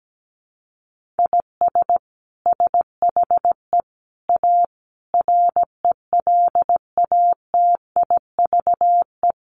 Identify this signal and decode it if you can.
Morse code, heard as a keyed tone, about 17 wpm, 700 Hz. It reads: IS SHE A RELATIVE
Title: Sentences from Top 1400 Words 17wpm